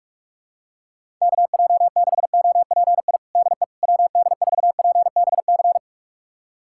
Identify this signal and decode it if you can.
Morse code, heard as a keyed tone, about 41 wpm, 700 Hz. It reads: KJ6OPI DE WD4PBC
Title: USA Callsign DE Callsign - 6 Characters - 45wpm